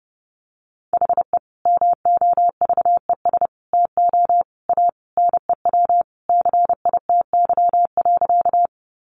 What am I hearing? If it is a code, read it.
Morse code, heard as a keyed tone, about 30 wpm, 700 Hz. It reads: HE MOVES TO A NEW CITY.
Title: Easy Phrases 30wpm